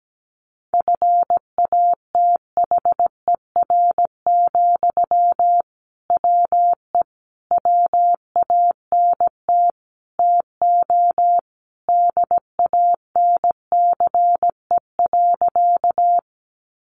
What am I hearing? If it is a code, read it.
Morse code, heard as a keyed tone, about 17 wpm, 700 Hz. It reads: FATHER, WE WANT TO DANCE.
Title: Sentences from Top 700 Words 17wpm